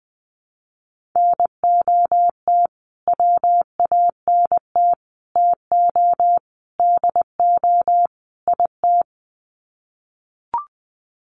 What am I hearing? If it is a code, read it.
Morse code, heard as a keyed tone, about 20 wpm, 700 Hz. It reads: NOT WANT TO DO IT
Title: Sets of 5 Words Encore 20wpm